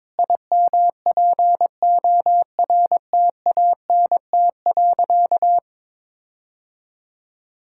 Morse code, heard as a keyed tone, about 22 wpm, 700 Hz. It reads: IMPORTANT.
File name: practice / 22wpm / Sentences with 8-Word Phrases - 22wpm